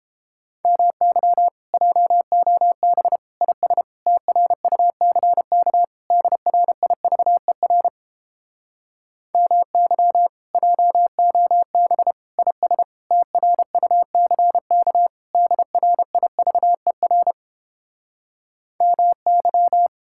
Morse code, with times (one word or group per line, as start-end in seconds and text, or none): 0.6-1.5 MY
1.7-3.2 JOB
3.4-3.8 IS
4.1-5.8 TRUCK
6.1-7.9 DRIVER
9.3-10.3 MY
10.5-12.1 JOB
12.4-12.8 IS
13.1-15.1 TRUCK
15.3-17.3 DRIVER
18.8-19.9 MY